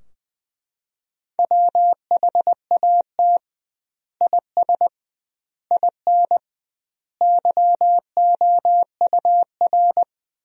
Morse code, heard as a keyed tone, about 21 wpm, 700 Hz. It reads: WHAT IS IN YOUR
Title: 2x Word-Spacing - Sentences from Top 200 Words - 20wpm